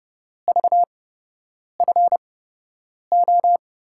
Morse code, heard as a keyed tone, about 30 wpm, spaced as for 8 wpm, 700 Hz. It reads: VFO
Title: USA Call Signs - 5 Characters - 8x Character Spacing - 30wpm